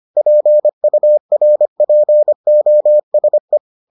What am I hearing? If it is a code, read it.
Morse code, heard as a keyed tone, about 25 wpm, 600 Hz. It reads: PURPOSE